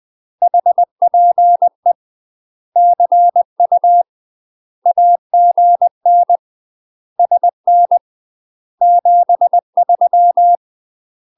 Morse code, heard as a keyed tone, about 20 wpm, 700 Hz. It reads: HPE CU AGN SN 73
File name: UK MASTER QSOs TRAINING / 20WPM / UK MASTER QSOs 0023